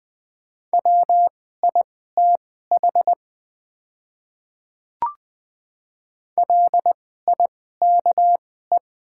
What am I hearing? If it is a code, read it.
Morse code, heard as a keyed tone, about 20 wpm, spaced as for 14 wpm, 700 Hz. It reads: WITH LIKE
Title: 2x Character Spacing - Top 100 Words - 20wpm